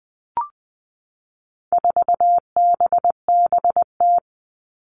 Morse code, heard as a keyed tone, about 20 wpm, 700 Hz. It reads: 4BBT
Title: All Cut Numbers - 4-Digit 20wpm